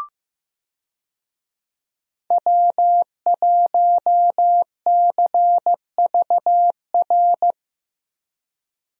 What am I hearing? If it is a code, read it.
Morse code, heard as a keyed tone, about 15 wpm, 700 Hz. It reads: W1CVR
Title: USA Call Signs - 5 Characters - 15wpm